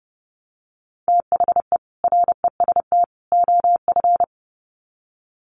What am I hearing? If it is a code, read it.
Morse code, heard as a keyed tone, about 30 wpm, 700 Hz. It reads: THE REST OF
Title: Sets of 3 Words 30wpm